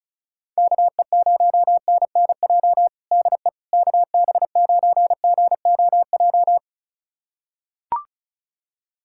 Morse code, heard as a keyed tone, about 36 wpm, 700 Hz. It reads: KE0NNJ DE KB9GOJ
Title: USA Callsign DE Callsign - 6 Characters - 35wpm